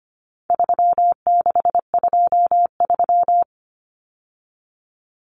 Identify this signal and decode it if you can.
Morse code, heard as a keyed tone, about 25 wpm, 700 Hz. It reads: 3623
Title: Numbers - 4-Digit 25wpm